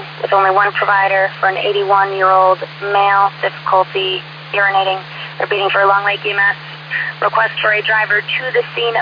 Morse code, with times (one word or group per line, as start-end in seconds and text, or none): none